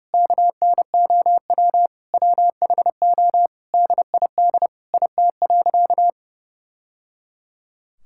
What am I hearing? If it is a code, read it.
Morse code, heard as a keyed tone, about 30 wpm, 700 Hz. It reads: KNOW WHO DID IT.
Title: Sentences from Top 1000 Words - Review Part 6 - 30wpm